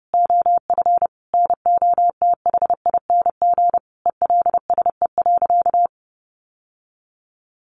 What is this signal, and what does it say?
Morse code, heard as a keyed tone, about 30 wpm, 700 Hz. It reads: OF NOTHING ELSE.
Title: Sentences from Top 800 Words 30wpm